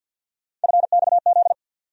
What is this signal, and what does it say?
Morse code, heard as a keyed tone, about 47 wpm, 700 Hz. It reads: FXZ
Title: USA Call Sign Postfixes - 3 Letters 50wpm